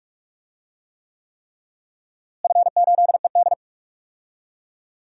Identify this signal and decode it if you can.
Morse code, heard as a keyed tone, about 46 wpm, 700 Hz. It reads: F8ED
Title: DXCC-100 Call Signs - 4 Characters - Repeated 3x - 45wpm